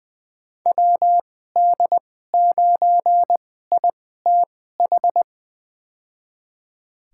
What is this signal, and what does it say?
Morse code, heard as a keyed tone, about 20 wpm, spaced as for 14 wpm, 700 Hz. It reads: WD9ITH
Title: USA Call Signs - 6 Characters - 2x Character Spacing - 20wpm